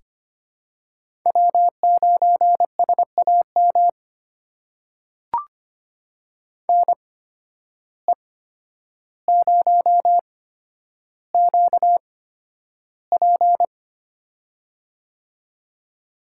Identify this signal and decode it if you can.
Morse code, heard as a keyed tone, about 25 wpm, spaced as for 11 wpm, 700 Hz. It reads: W9SAM NE0QP
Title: USA Call Signs - 5 Characters - 8x Character Spacing - 25wpm